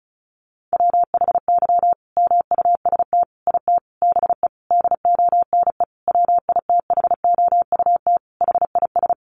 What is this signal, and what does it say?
Morse code, heard as a keyed tone, about 34 wpm, 700 Hz. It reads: WHY MUST IT BE DONE WITHOUT HIS
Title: Sentences from Top 600 Words 35wpm